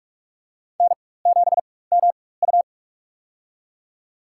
Morse code, heard as a keyed tone, about 47 wpm, spaced as for 21 wpm, 700 Hz. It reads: N7MU
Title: USA Call Signs - 4x Character Spacing - 45wpm